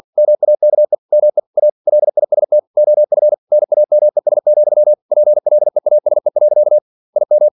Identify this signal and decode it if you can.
Morse code, heard as a keyed tone, about 51 wpm, 600 Hz. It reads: MAKE ME A LIST OF NAMES, PLEASE. IM
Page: Never 47wpm